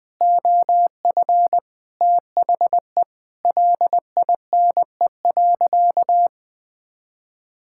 Morse code, heard as a keyed tone, about 20 wpm, 700 Hz. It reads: OF THE LINE.
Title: Sentences with 5-Word Phrases - 20wpm